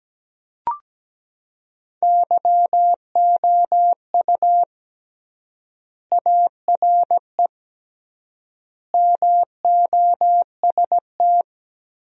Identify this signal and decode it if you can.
Morse code, heard as a keyed tone, about 17 wpm, 700 Hz. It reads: YOU ARE MOST
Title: 3x Word-Spacing - Sentences from Top 200 Words - 17wpm